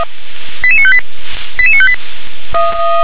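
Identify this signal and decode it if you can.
Five-tone selective call (ZVEI), heard as quick runs of five tones, tones of about 70 ms. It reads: 88958 88958
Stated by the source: ZVEI five-tone selective call received on an SDR.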